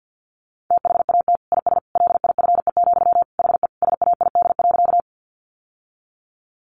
Morse code, heard as a keyed tone, about 50 wpm, 700 Hz. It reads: THAT IS LIFE, HE SAID.
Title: Sentences from Top 900 Words 50wpm